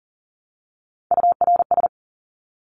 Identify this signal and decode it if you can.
Morse code, heard as a keyed tone, about 40 wpm, 700 Hz. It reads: URS
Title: Abbreviations Repeated 3x - 40wpm